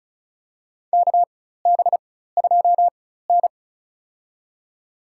Morse code, heard as a keyed tone, about 36 wpm, spaced as for 16 wpm, 700 Hz. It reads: KB2N
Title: USA Call Signs - 4 Characters - 4x Character Spacing - 35wpm